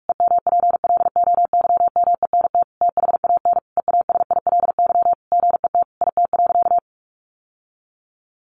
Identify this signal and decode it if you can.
Morse code, heard as a keyed tone, about 44 wpm, 700 Hz. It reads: EMPLOYMENT THAN EASILY GET IT.
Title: Sentences from Top 1500 Words 45wpm